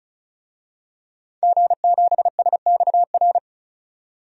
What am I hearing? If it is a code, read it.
Morse code, heard as a keyed tone, about 34 wpm, 700 Hz. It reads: G7SXR
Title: DXCC-100 Call Signs - 5 Characters - 35wpm